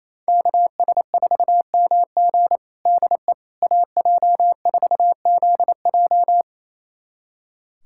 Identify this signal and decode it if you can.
Morse code, heard as a keyed tone, about 28 wpm, 700 Hz. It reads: KS4MG DE AJ4ZJ
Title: USA Callsign DE Callsign - 5 Characters - 28wpm